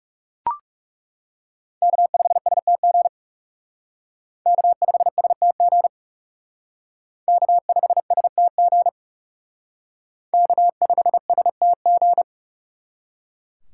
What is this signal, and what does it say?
Morse code, heard as a keyed tone, about 36 wpm, 700 Hz. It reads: K5STG K5STG K5STG K5STG
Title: Speed Racing - USA 1x3 Call Signs 30wpm